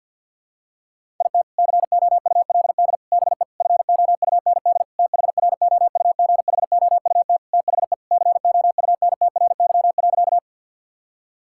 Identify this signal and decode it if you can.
Morse code, heard as a keyed tone, about 52 wpm, 700 Hz. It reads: IT COULD BE FOUND THROUGHOUT THE COUNTRY.
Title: Sentences from Top 1000 Words - Review Part 5 - 50wpm